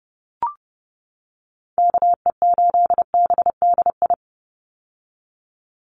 Morse code, heard as a keyed tone, about 30 wpm, 700 Hz. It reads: KE8BDI
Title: USA Call Signs 30wpm